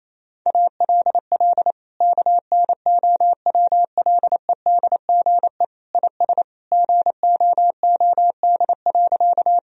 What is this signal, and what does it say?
Morse code, heard as a keyed tone, about 28 wpm, 700 Hz. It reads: ALL KNOWLEDGE IS GOOD.
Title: Sentences from Top 1000 Words - Review Part 3 - 28wpm